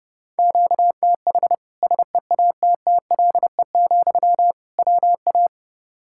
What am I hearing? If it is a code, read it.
Morse code, heard as a keyed tone, about 30 wpm, 700 Hz. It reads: QTH SEATTLE, WA
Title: Practice QSO - Exchange 1 - Beginning - 30wpm